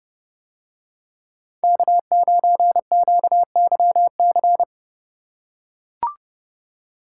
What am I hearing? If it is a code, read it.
Morse code, heard as a keyed tone, about 28 wpm, 700 Hz. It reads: K9QYC